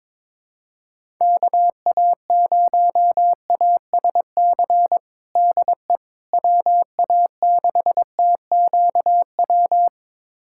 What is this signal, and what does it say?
Morse code, heard as a keyed tone, about 22 wpm, 700 Hz. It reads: KA0ASC DE WA6TQW
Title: USA Callsign DE Callsign - 6 Characters - 22wpm